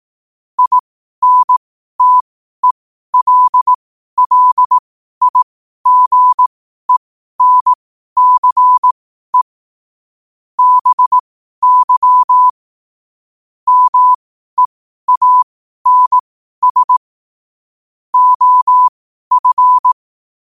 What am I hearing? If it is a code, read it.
Morse code, heard as a keyed tone, about 18 wpm, spaced as for 12 wpm, 1000 Hz. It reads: INTELLIGENCE BY MEANS OF
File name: qotdfiles / 12wpm